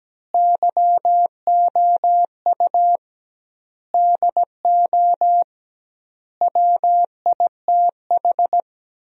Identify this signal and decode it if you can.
Morse code, heard as a keyed tone, about 17 wpm, 700 Hz. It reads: YOU DO WITH
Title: 2x Word-Spacing - Sentences from Top 300 Words - 17wpm